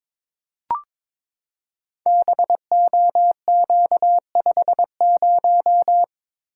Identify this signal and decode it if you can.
Morse code, heard as a keyed tone, about 22 wpm, 700 Hz. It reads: BOQ50
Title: Code Groups - 5 Letter:Numbers - 22wpm